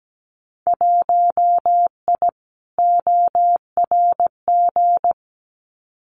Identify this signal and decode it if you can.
Morse code, heard as a keyed tone, about 17 wpm, 700 Hz. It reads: 1I ORG
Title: Winter Field Day Exchanges Encore 17wpm